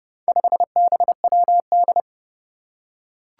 Morse code, heard as a keyed tone, about 30 wpm, 700 Hz. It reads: 5BWD
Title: Four Challenging Letter-Number Combinations 30wpm